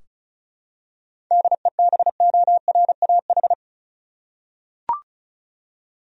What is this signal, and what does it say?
Morse code, heard as a keyed tone, about 34 wpm, 700 Hz. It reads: DEBORAH
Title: USA Common Names - 35wpm